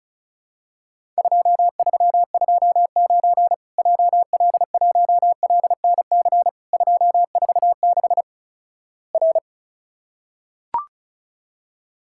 Morse code, heard as a keyed tone, about 36 wpm, 700 Hz. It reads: 2329 JL1LNC 246 R
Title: Worked All Europe DX Contest - QTC Exchange - 35wpm